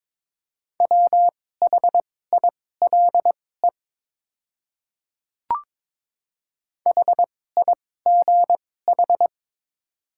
Morse code, heard as a keyed tone, about 22 wpm, spaced as for 16 wpm, 700 Hz. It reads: WHILE HIGH
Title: 2x Character Spacing - Top 300 Words - 22wpm